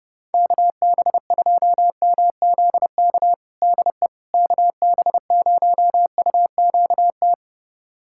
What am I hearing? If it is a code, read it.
Morse code, heard as a keyed tone, about 32 wpm, 700 Hz. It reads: KB2MZK DE KB0UQT
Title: USA Callsign DE Callsign 30wpm